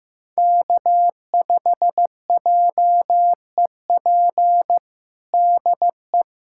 Morse code, heard as a keyed tone, about 15 wpm, 700 Hz. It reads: K5JEP DE
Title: USA Callsign DE Callsign - 5 Characters - 15wpm